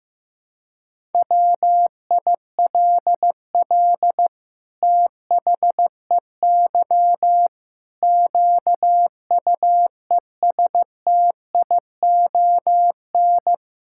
Morse code, heard as a keyed tone, about 15 wpm, 700 Hz. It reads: WILL THEY QUESTION
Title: Sentences from Top 1000 Words - Review Part 5 - 15wpm